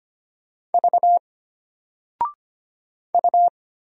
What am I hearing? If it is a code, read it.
Morse code, heard as a keyed tone, about 25 wpm, 700 Hz. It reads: V U
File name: Single-Letter-Number-25wpm